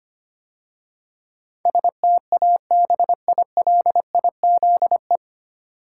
Morse code, heard as a keyed tone, about 25 wpm, 700 Hz. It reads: STABILIZE
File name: practice / 25wpm / 9 Letter Words - First 3 Letters Given - 25wpm